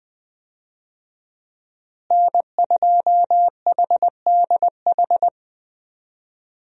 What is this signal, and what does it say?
Morse code, heard as a keyed tone, about 20 wpm, 700 Hz. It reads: N2HDH
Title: USA Call Signs - 5 Characters - Repeated 2x - 20wpm